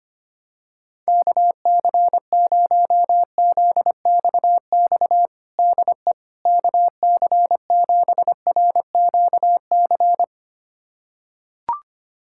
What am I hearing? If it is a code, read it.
Morse code, heard as a keyed tone, about 25 wpm, 700 Hz. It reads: KC0ZXX DE KC7RQC
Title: USA Callsign DE Callsign 25wpm